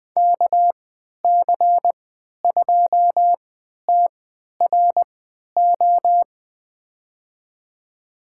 Morse code, heard as a keyed tone, about 20 wpm, spaced as for 11 wpm, 700 Hz. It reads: KC2TRO
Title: USA Call Signs - 6 Characters - 3x Character Spacing - 20wpm